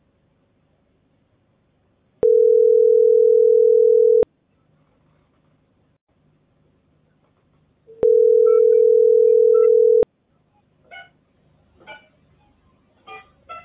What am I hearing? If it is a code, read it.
Ringback tone (North America), 440 + 480 Hz.